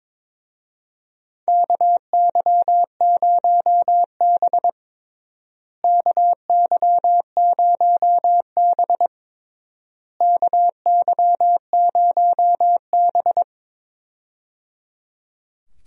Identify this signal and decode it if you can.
Morse code, heard as a keyed tone, about 22 wpm, 700 Hz. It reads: KY0B KY0B KY0B
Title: USA Call Signs - 4 Characters - Repeated 3x - 22wpm